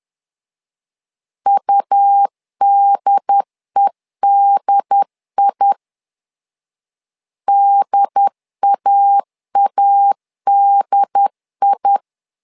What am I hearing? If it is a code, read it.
Morse code, heard as a keyed tone, about 10 wpm, 800 Hz. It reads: UDEDI DAADI
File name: LEZIONI / 2 UHDN / 2 UHD 10wpm 1826